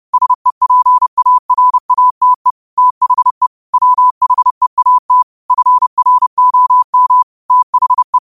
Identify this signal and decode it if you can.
Morse code, heard as a keyed tone, about 30 wpm, 1000 Hz. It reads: SEPARATE THE WHEAT FROM THE